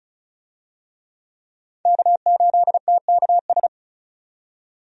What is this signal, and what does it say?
Morse code, heard as a keyed tone, about 36 wpm, 700 Hz. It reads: K8TKS
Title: USA Call Signs - 5 Characters - 35wpm